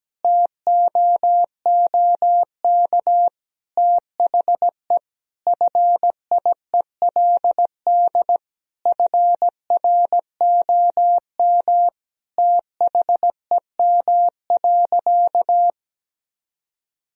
Morse code, heard as a keyed tone, about 17 wpm, 700 Hz. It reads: TOOK THE FIELD FROM THEM.
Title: Sentences from Top 1000 Words - Review Part 2 - 17wpm